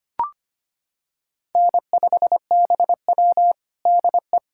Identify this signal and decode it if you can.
Morse code, heard as a keyed tone, about 25 wpm, 700 Hz. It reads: N5BW DE
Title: USA Callsign DE Callsign - 4 Characters - 25wpm